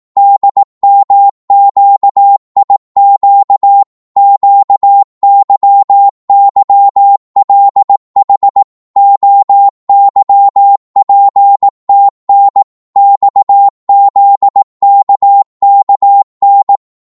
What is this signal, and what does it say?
Morse code, heard as a keyed tone, about 18 wpm, 800 Hz. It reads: DMQIQ QYYLH OYPTN XZKKN